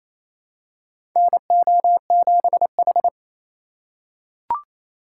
Morse code, heard as a keyed tone, about 28 wpm, 700 Hz. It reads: NO7H
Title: USA Call Signs - 4 Characters - 2x Character Spacing - 28wpm